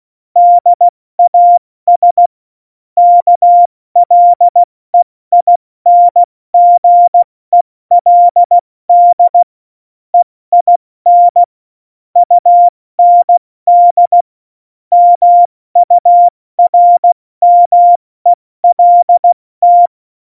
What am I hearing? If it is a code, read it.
Morse code, heard as a keyed tone, about 16 wpm, spaced as for 14 wpm, 700 Hz. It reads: DAS KLEINGELD EIN UND MURMELT
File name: Morsekurs / Witze / Witz 15. Oktober 2019, 70 F80